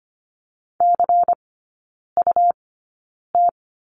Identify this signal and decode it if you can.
Morse code, heard as a keyed tone, about 25 wpm, spaced as for 9 wpm, 700 Hz. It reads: CUT
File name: course / 25:10wpm / 100 Lesson 17e - Letter u - Three Characters 25wpm